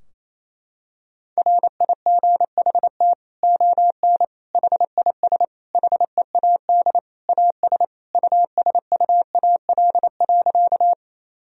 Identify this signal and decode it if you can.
Morse code, heard as a keyed tone, about 28 wpm, 700 Hz. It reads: RIGHT ON HIS HEAD AS USUAL.
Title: Sentences from Top 1000 Words - Review Part 2 - 28wpm